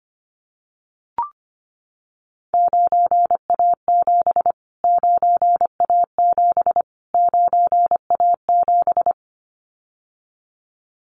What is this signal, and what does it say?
Morse code, heard as a keyed tone, about 25 wpm, 700 Hz. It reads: 9A7 9A7 9A7
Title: Callsign Prefixes for Top 25 DXCC - Repeated 3x - 25wpm